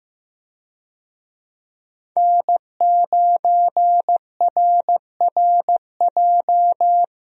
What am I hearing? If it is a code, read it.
Morse code, heard as a keyed tone, about 15 wpm, 700 Hz. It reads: N9RRJ
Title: USA Call Signs - 5 Characters - 15wpm